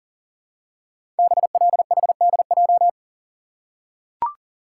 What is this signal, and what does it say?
Morse code, heard as a keyed tone, about 42 wpm, 700 Hz. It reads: BLHDJ
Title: Code Groups - 5 Letters - 40wpm